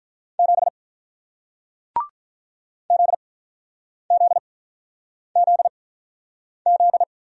Morse code, heard as a keyed tone, about 47 wpm, spaced as for 9 wpm, 700 Hz. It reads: 7 ZZZZ